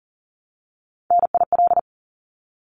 Morse code, heard as a keyed tone, about 40 wpm, 700 Hz. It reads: NIL
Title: Abbreviations 40wpm